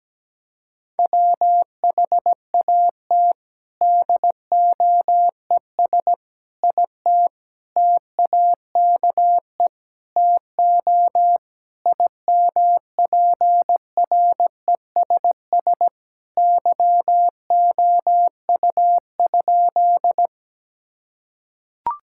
Morse code, heard as a keyed tone, about 17 wpm, 700 Hz. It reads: WHAT DOES IT TAKE TO IMPRESS YOU?
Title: Sentences from Top 1600 Words 17wpm